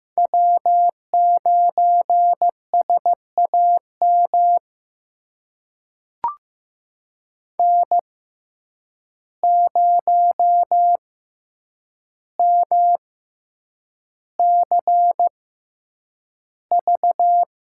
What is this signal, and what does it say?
Morse code, heard as a keyed tone, about 15 wpm, spaced as for 8 wpm, 700 Hz. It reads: W9SAM N0MCV
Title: USA Call Signs - 5 Characters - 6x Character Spacing - 15wpm